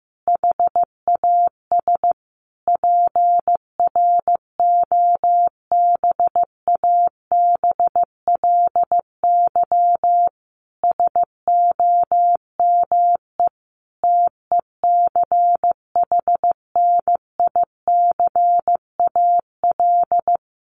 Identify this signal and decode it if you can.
Morse code, heard as a keyed tone, about 14 wpm, 700 Hz. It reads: HAS PROBABLY SOME TECHNICAL
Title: Sentences from Top 1100 Words 15wpm